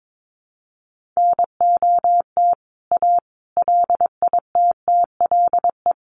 Morse code, heard as a keyed tone, about 22 wpm, 700 Hz. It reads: NOT A LITTLE
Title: A Sound Of Thunder 22wpm